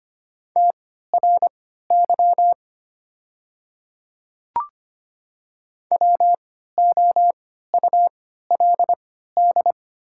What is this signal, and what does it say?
Morse code, heard as a keyed tone, about 25 wpm, spaced as for 14 wpm, 700 Hz. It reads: TRY WOULD
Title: 3x Character Spacing - Top 200 Words - 25wpm